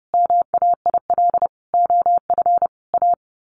Morse code, heard as a keyed tone, about 30 wpm, 700 Hz. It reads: MAIL OF A
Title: A Sound Of Thunder 30wpm